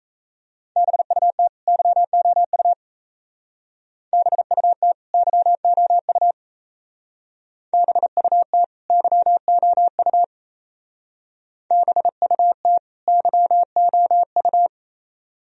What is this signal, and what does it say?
Morse code, heard as a keyed tone, about 36 wpm, 700 Hz. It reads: BUT YOU BUT YOU BUT YOU BUT YOU